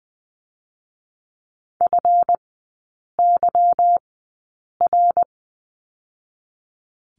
Morse code, heard as a keyed tone, about 20 wpm, 700 Hz. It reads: F Y R
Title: 145 Lesson 24e - Letter y - Three Characters 20wpm